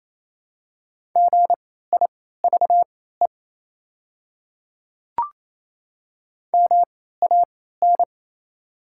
Morse code, heard as a keyed tone, about 28 wpm, spaced as for 16 wpm, 700 Hz. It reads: GIVE MAN
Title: 3x Character Spacing - Top 200 Words - 28wpm